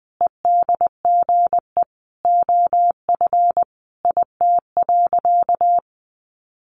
Morse code, heard as a keyed tone, about 20 wpm, 700 Hz. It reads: EDGE OF IT.
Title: Sentences from Top 1300 Words 20wpm